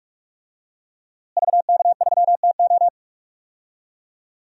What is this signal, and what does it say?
Morse code, heard as a keyed tone, about 46 wpm, 700 Hz. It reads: VK3TO
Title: DXCC-100 Call Signs - 5 Characters - Repeated 3x - 45wpm